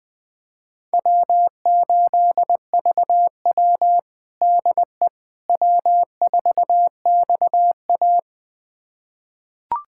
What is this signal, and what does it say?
Morse code, heard as a keyed tone, about 20 wpm, 700 Hz. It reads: W8VW DE W4XA
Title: USA Callsign DE Callsign - 4 Characters - 20wpm